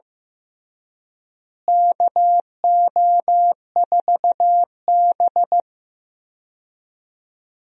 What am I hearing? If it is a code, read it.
Morse code, heard as a keyed tone, about 15 wpm, 700 Hz. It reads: KO4B
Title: USA Call Signs - 4 Characters - Repeated 3x - 15wpm